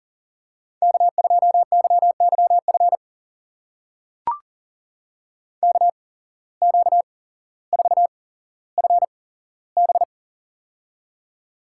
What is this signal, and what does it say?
Morse code, heard as a keyed tone, about 40 wpm, spaced as for 17 wpm, 700 Hz. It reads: K2YYF KQ4FB
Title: USA Call Signs - 5 Characters - 8x Character Spacing - 40wpm